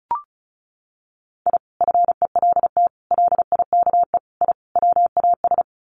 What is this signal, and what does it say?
Morse code, heard as a keyed tone, about 34 wpm, 700 Hz. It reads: I FELT LIKE I WAS
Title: Sets of 5 Words Encore 35wpm